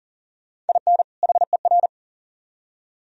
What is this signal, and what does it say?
Morse code, heard as a keyed tone, about 52 wpm, 700 Hz. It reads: IN HER